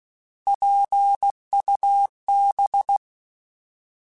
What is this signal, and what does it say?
Morse code, heard as a keyed tone, about 16 wpm, 800 Hz. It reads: PUB
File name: Alphabetical / Alphabetical mp3s / Ps @15